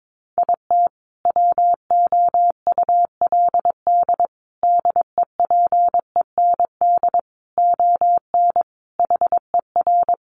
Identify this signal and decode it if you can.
Morse code, heard as a keyed tone, about 22 wpm, 700 Hz. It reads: IT WOULD DEPEND ON HER
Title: Sentences from Top 1300 Words 22wpm